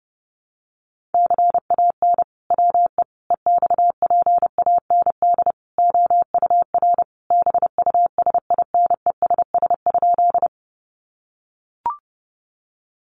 Morse code, heard as a keyed tone, about 30 wpm, 700 Hz. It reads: CAN WE EXPAND OUR BUSINESS?
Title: Sentences from Top 1400 Words 30wpm